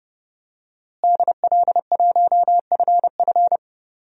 Morse code, heard as a keyed tone, about 30 wpm, 700 Hz. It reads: DL1FF
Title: DXCC-100 Call Signs - 5 Characters - Repeated 2x - 30wpm